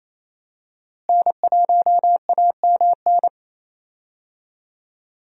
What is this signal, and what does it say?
Morse code, heard as a keyed tone, about 28 wpm, 700 Hz. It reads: N1AMN
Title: USA Call Signs - 5 Characters - Repeated 3x - 28wpm